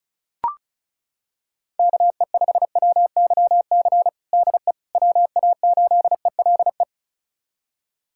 Morse code, heard as a keyed tone, about 34 wpm, 690 Hz. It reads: KE5WYC DE WA8ELE
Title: USA Callsign DE Callsign - 6 Characters - 35wpm